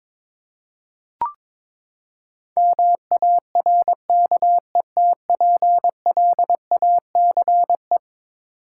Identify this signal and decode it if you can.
Morse code, heard as a keyed tone, about 21 wpm, 700 Hz. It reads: MARKETPLACE
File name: practice / 22wpm / Eleven Letter Words - 22wpm